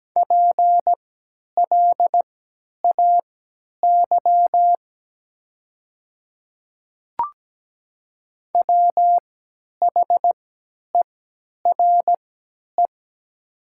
Morse code, heard as a keyed tone, about 17 wpm, spaced as for 10 wpm, 700 Hz. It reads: PLAY WHERE
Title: 3x Character Spacing - Top 200 Words - 17wpm